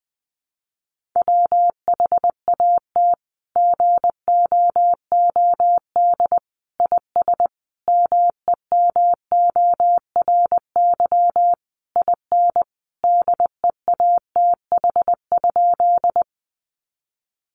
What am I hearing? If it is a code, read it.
Morse code, heard as a keyed tone, about 19 wpm, 700 Hz. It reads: WHAT GOOD IS MEMORY IN DEATH?
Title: Sentences from Top 1300 Words 20wpm